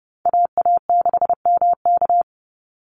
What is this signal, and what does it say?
Morse code, heard as a keyed tone, about 30 wpm, 700 Hz. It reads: AA6MK
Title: Call Signs Repeated 2x - 30wpm